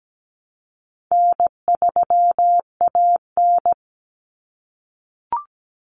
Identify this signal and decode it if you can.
Morse code, heard as a keyed tone, about 17 wpm, 700 Hz. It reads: N3AN
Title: Common Cut Numbers - 4-Digit 17wpm